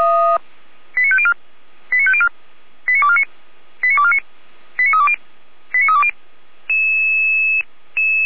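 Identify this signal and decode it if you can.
Five-tone selective call (ZVEI), heard as quick runs of five tones, tones of about 70 ms. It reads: 89504 89504 89249 89249 89320 89320
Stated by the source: Five-tone ZVEI selective call received by SDR, from audio.f85360.feuerwehr-orlamuende.org (Feuerwehr Orlamünde).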